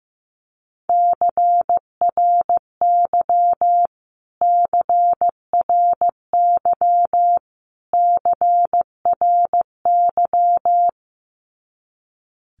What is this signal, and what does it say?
Morse code, heard as a keyed tone, about 15 wpm, 700 Hz. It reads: CRY CRY CRY
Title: Top 500 Words Repeated 15wpm